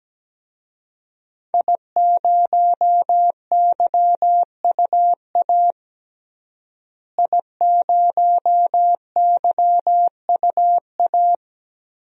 Morse code, heard as a keyed tone, about 17 wpm, 700 Hz. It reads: I0YUA I0YUA
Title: DXCC-100 Call Signs - 5 Characters - Repeated 3x - 17wpm